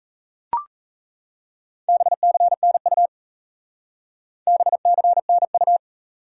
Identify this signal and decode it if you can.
Morse code, heard as a keyed tone, about 36 wpm, 700 Hz. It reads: BCNU BCNU